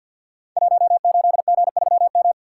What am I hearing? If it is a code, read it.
Morse code, heard as a keyed tone, about 52 wpm, 700 Hz. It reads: 18G3M
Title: Code Groups - 5 Letter:Numbers - 50wpm